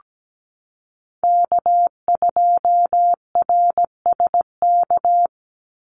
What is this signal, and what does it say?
Morse code, heard as a keyed tone, about 17 wpm, 700 Hz. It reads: K2RSK
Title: Call Signs - 17wpm